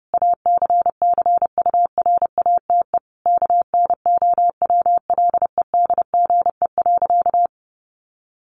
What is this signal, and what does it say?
Morse code, heard as a keyed tone, about 30 wpm, 700 Hz. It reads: ACCURATE KNOWLEDGE.
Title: Sentences from Top 1200 Words 30wpm